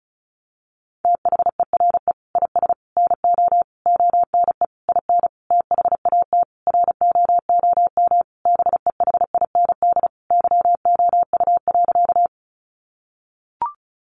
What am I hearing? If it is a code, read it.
Morse code, heard as a keyed tone, about 34 wpm, 700 Hz. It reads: THERE IS NO ONE IN THAT ROOM BEHIND YOU.
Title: Sentences from Top 500 Words 35wpm